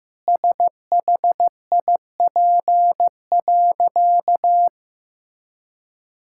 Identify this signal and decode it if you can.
Morse code, heard as a keyed tone, about 15 wpm, 700 Hz. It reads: SHIP.
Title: Sentences from Top 1000 Words - Review Part 5 - 15wpm